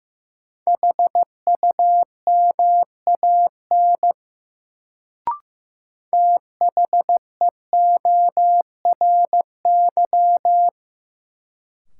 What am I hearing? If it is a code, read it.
Morse code, heard as a keyed tone, about 15 wpm, 700 Hz. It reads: HUMAN THEORY
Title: Sentences from Top 600 Words - New Word Drill - 15wpm